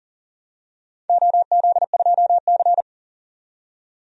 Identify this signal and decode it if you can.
Morse code, heard as a keyed tone, about 40 wpm, 700 Hz. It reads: OZ2C
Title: DXCC-100 Call Signs - 4 Characters - 40wpm